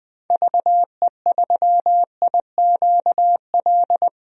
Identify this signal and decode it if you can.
Morse code, heard as a keyed tone, about 20 wpm, 700 Hz. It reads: VE3IQL
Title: DXCC-100 Call Signs - 6 Characters - 20wpm